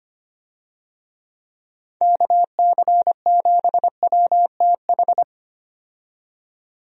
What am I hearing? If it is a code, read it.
Morse code, heard as a keyed tone, about 25 wpm, 700 Hz. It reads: KC7WTH
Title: USA Call Signs - 6 Characters - 25wpm